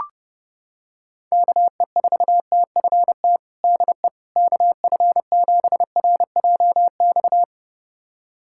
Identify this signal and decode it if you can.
Morse code, heard as a keyed tone, about 30 wpm, 700 Hz. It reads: KE4TFT DE KF7RJX
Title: USA Callsign DE Callsign - 6 Characters - 30wpm